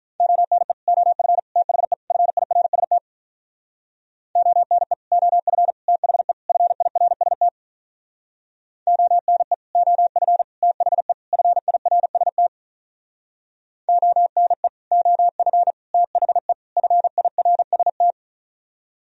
Morse code, heard as a keyed tone, about 45 wpm, 700 Hz. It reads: ONE OF THE FIRST ONE OF THE FIRST ONE OF THE FIRST ONE OF THE FIRST